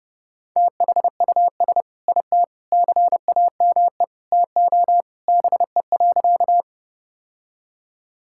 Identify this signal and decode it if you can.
Morse code, heard as a keyed tone, about 30 wpm, 700 Hz. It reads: THUS IT CAME TO BE.
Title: Sentences from Top 1000 Words - Review Part 5 - 30wpm